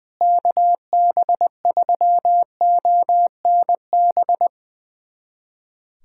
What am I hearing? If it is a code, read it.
Morse code, heard as a keyed tone, about 21 wpm, 700 Hz. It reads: KB3ONB
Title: USA Callsign DE Callsign - 6 Characters - 20wpm